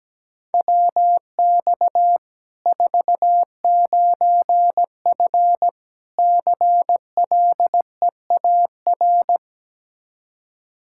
Morse code, heard as a keyed tone, about 17 wpm, 700 Hz. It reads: WX 49F CLEAR
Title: Practice QSO - Exchange 2 - Info - 17wpm